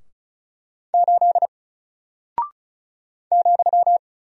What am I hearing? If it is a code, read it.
Morse code, heard as a keyed tone, about 36 wpm, 700 Hz. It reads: 8 ,
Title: Single Character (Letter:Number:Punctuation:Prosign) Alt 35wpm